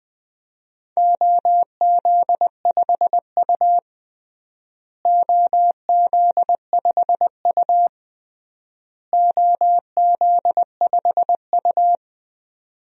Morse code, heard as a keyed tone, about 21 wpm, 700 Hz. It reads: OZ5U OZ5U OZ5U
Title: DXCC-100 Call Signs - 4 Characters - Repeated 3x - 20wpm